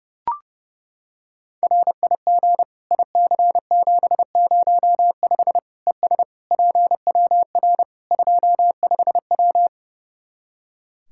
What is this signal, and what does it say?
Morse code, heard as a keyed tone, about 30 wpm, 700 Hz. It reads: RIG IC705 ES PWR 25W
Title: Practice QSO - Exchange 2 - Info - 30wpm